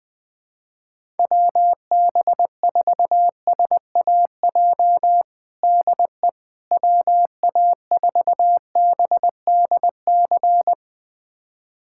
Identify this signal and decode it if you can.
Morse code, heard as a keyed tone, about 20 wpm, 700 Hz. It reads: WB4SAJ DE WA4BDC
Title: USA Callsign DE Callsign - 6 Characters - 20wpm